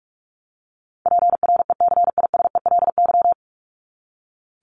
Morse code, heard as a keyed tone, about 44 wpm, 700 Hz. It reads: PRECISELY
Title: LY Suffix - Top 100 Words with Base Words 45wpm